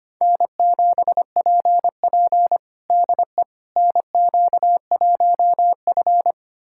Morse code, heard as a keyed tone, about 25 wpm, 700 Hz. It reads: N7PP DE NQ1F
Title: USA Callsign DE Callsign - 4 Characters - 25wpm